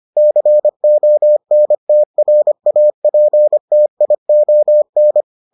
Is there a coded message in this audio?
Yes: Morse code from 0.2 to 5.2 s, heard as a keyed tone, about 25 wpm, 600 Hz. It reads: CONTRAPTION